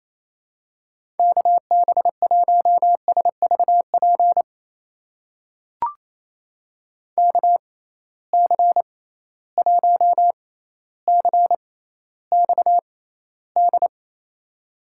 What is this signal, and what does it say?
Morse code, heard as a keyed tone, about 28 wpm, spaced as for 14 wpm, 700 Hz. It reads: KB1SVP KC1CXD
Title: USA Call Signs - 6 Characters - 6x Character Spacing - 28wpm